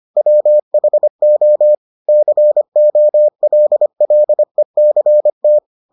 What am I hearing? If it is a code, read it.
Morse code, heard as a keyed tone, about 25 wpm, 600 Hz. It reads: WHO COLLECT